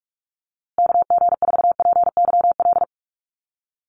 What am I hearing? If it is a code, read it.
Morse code, heard as a keyed tone, about 44 wpm, 700 Hz. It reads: KG4PYL
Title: USA Call Signs Repeated 2x 45wpm